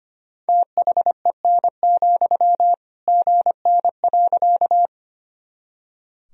Morse code, heard as a keyed tone, about 25 wpm, 700 Hz. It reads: THEN, GN.
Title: Sentences from Top 300 Words and Abbreviations 25wpm